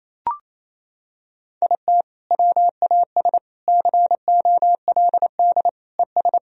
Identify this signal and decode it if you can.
Morse code, heard as a keyed tone, about 28 wpm, 700 Hz. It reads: IT WAS COLD ES
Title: Sentences from Top 500 Words and Abbreviations 28wpm